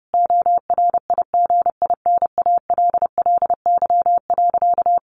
Morse code, heard as a keyed tone, about 30 wpm, 700 Hz. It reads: ORIGINALLY.
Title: Sentences from Top 1500 Words 30wpm